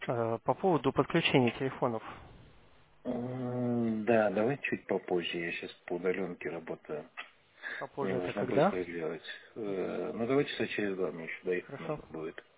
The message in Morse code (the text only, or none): none